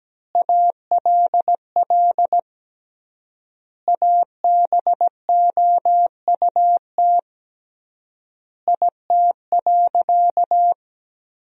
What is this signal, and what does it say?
Morse code, heard as a keyed tone, about 17 wpm, 700 Hz. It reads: ALL ABOUT IT.
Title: 3x Word-Spacing - Sentences from Top 400 Words - 17wpm